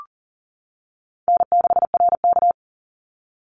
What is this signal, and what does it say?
Morse code, heard as a keyed tone, about 38 wpm, 700 Hz. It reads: N6RK
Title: Call Signs - 40wpm